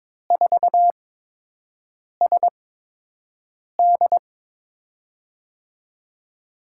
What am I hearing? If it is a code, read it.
Morse code, heard as a keyed tone, about 22 wpm, spaced as for 6 wpm, 700 Hz. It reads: 4SD
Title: USA Call Signs - 5 Characters - 8x Character Spacing - 22wpm